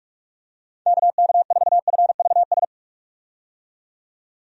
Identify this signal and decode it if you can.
Morse code, heard as a keyed tone, about 43 wpm, 700 Hz. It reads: KK4FVS
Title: USA Call Signs - 6 Characters - 45wpm